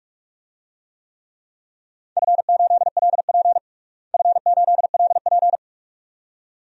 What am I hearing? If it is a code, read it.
Morse code, heard as a keyed tone, about 41 wpm, 700 Hz. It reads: F8LP F8LP
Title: DXCC-100 Call Signs - 4 Characters - Repeated 2x - 45wpm